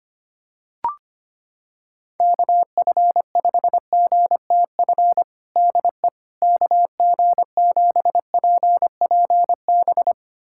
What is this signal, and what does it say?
Morse code, heard as a keyed tone, about 25 wpm, 700 Hz. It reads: KF5GTF DE KG7PPB
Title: USA Callsign DE Callsign 25wpm